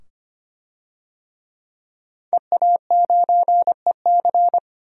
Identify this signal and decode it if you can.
Morse code, heard as a keyed tone, about 25 wpm, 700 Hz. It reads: EA9EC
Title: DXCC-100 Call Signs - 5 Characters - 25wpm